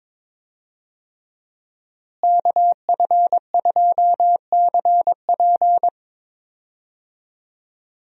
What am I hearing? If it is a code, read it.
Morse code, heard as a keyed tone, about 22 wpm, 700 Hz. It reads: KF2CP
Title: USA Call Signs - 5 Characters - Repeated 3x - 22wpm